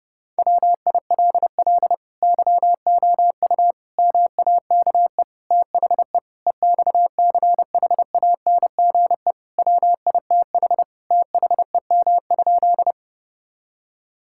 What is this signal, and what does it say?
Morse code, heard as a keyed tone, about 32 wpm, 700 Hz. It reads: WILL YOU MAKE THE EXCHANGE WITH THEM?
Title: Sentences from Top 1000 Words - Review Part 6 - 30wpm